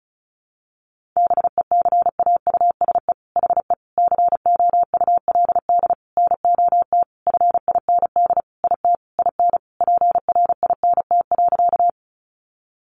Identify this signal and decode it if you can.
Morse code, heard as a keyed tone, about 34 wpm, 700 Hz. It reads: BECAUSE HE COULD NOT FIND IT IN PRINT.
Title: Sentences from Top 1400 Words 35wpm